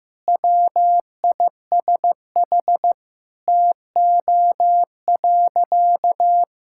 Morse code, heard as a keyed tone, about 16 wpm, 700 Hz. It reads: WISH TO.
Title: Sentences from Top 1600 Words 15wpm